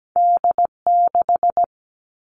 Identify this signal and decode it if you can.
Morse code, heard as a keyed tone, about 17 wpm, 700 Hz. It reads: D6
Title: All Cut Numbers - 2-Digit 17wpm